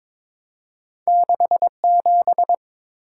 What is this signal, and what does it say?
Morse code, heard as a keyed tone, about 22 wpm, 700 Hz. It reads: 67
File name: Two Letter-Number Combinations - 22wpm